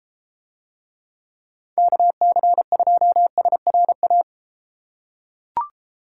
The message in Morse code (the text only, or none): KC2SRA